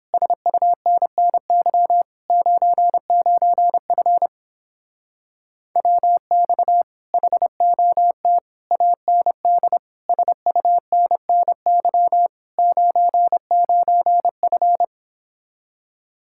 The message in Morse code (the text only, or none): SUNNY 99F WX HOT AND SUNNY 99F